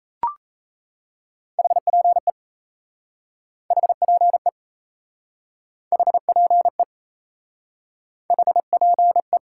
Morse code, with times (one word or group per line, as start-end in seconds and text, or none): 1.6-2.3 HPE
3.7-4.5 HPE
5.9-6.8 HPE
8.3-9.4 HPE